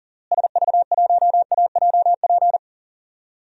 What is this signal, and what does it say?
Morse code, heard as a keyed tone, about 40 wpm, 700 Hz. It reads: SV1AJP
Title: DXCC-100 Call Signs - 6 Characters - Repeated 3x - 40wpm